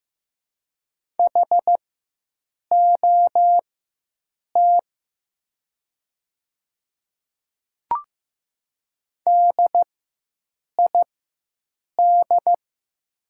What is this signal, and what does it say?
Morse code, heard as a keyed tone, about 15 wpm, spaced as for 7 wpm, 700 Hz. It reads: HOT DID
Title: 4x Character Spacing - Top 200 Words - 15wpm